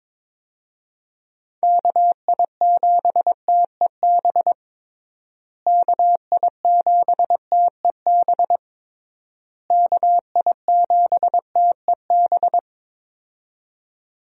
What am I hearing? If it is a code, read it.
Morse code, heard as a keyed tone, about 22 wpm, 700 Hz. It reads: KI7TEB KI7TEB KI7TEB
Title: USA Call Signs - 6 Characters - Repeated 3x - 22wpm